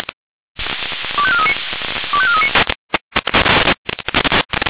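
Five-tone selective call (ZVEI), heard as quick runs of five tones, tones of about 70 ms. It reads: 25439 25439
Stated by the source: ZVEI five-tone selective call received on an SDR.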